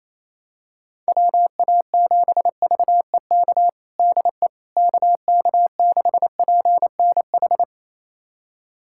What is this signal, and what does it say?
Morse code, heard as a keyed tone, about 28 wpm, 700 Hz. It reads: WA7VEK DE KK6PNH
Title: USA Callsign DE Callsign - 6 Characters - 28wpm